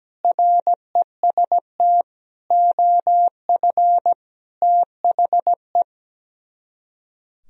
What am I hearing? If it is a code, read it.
Morse code, heard as a keyed tone, about 17 wpm, 700 Hz. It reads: REST OF THE
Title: Speed-Racing - Sets of 5 Words 17wpm